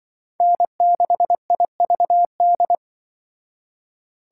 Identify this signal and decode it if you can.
Morse code, heard as a keyed tone, about 24 wpm, 700 Hz. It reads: N6IVD
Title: Speed Racing - USA 1x3 Call Signs 20wpm